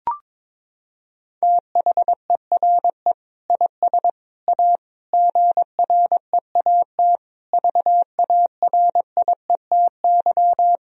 Morse code, heard as a keyed tone, about 22 wpm, 700 Hz. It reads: THERE IS A GREAT VARIETY
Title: Sentences from Top 1000 Words - Review Part 4 - 22wpm